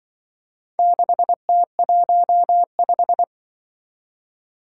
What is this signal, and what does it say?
Morse code, heard as a keyed tone, about 24 wpm, 700 Hz. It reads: 6T15
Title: Speed-Racing - Four Letter-Number Combinations - 20wpm